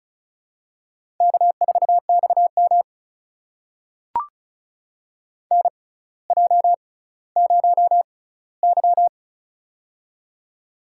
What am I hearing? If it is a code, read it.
Morse code, heard as a keyed tone, about 35 wpm, spaced as for 18 wpm, 700 Hz. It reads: K4XM NJ0Y
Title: USA Call Signs - 4 Characters - 6x Character Spacing - 35wpm